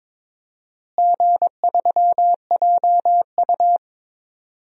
Morse code, heard as a keyed tone, about 22 wpm, 700 Hz. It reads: G3JU
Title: DXCC-100 Call Signs - 4 Characters - Repeated 3x - 22wpm